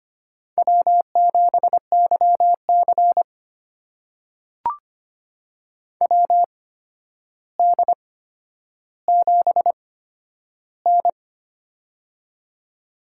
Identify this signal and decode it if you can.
Morse code, heard as a keyed tone, about 25 wpm, spaced as for 11 wpm, 700 Hz. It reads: W7YC WD7N
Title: USA Call Signs - 4 Characters - 8x Character Spacing - 25wpm